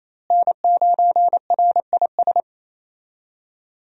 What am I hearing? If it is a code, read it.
Morse code, heard as a keyed tone, about 28 wpm, 700 Hz. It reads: N9RIS
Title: USA Call Signs - 5 Characters - Repeated 2x - 28wpm